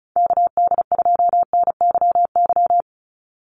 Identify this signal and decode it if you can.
Morse code, heard as a keyed tone, about 34 wpm, 700 Hz. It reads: KD2NYY
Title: USA Call Signs 35wpm